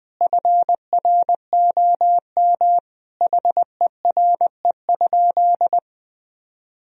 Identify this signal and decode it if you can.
Morse code, heard as a keyed tone, about 20 wpm, 700 Hz. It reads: FROM HERE?
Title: Sentences from Top 1000 Words - Review Part 4 - 20wpm